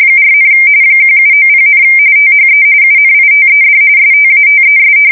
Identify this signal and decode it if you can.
RTTY, about 45.45 baud. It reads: 4, 1790, THE UNITED STATES
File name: fsk170 / usnavy1